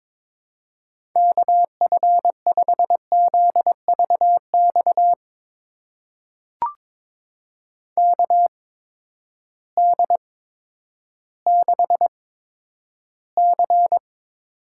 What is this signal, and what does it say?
Morse code, heard as a keyed tone, about 22 wpm, spaced as for 11 wpm, 700 Hz. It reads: KF5ZVX KD6C
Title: USA Call Signs - 6 Characters - 8x Character Spacing - 22wpm